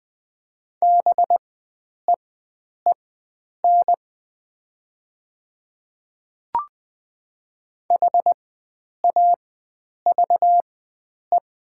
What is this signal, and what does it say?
Morse code, heard as a keyed tone, about 20 wpm, spaced as for 9 wpm, 690 Hz. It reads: BEEN HAVE
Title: 4x Character Spacing - Top 100 Words - 20wpm